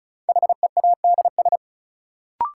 Morse code, heard as a keyed tone, about 36 wpm, 700 Hz. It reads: HEADS
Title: Sentences from Top 1000 Words - Review Part 3 - New Word Drill - 35wpm